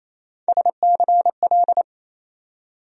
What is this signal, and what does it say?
Morse code, heard as a keyed tone, about 28 wpm, 700 Hz. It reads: SCL
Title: USA Call Sign Postfixes - 3 Letters 28wpm